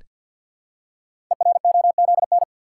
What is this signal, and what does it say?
Morse code, heard as a keyed tone, about 46 wpm, 700 Hz. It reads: EROZN
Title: Code Groups - 5 Letters - 50wpm